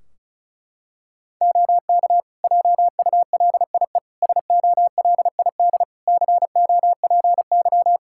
Morse code, heard as a keyed tone, about 36 wpm, 700 Hz. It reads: OK JULIE SOLID COPY
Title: Practice QSO - Exchange 3 - More Info - 35wpm